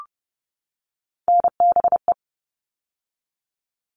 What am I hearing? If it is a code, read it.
Morse code, heard as a keyed tone, about 30 wpm, 700 Hz. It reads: NBE
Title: All Cut Numbers - 3-Digit 30wpm